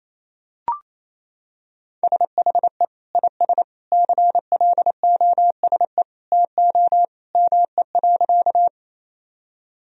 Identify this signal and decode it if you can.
Morse code, heard as a keyed tone, about 28 wpm, 700 Hz. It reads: SHE IS CLOSE TO ME.
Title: Sentences from Top 1000 Words - Review Part 1 - 28wpm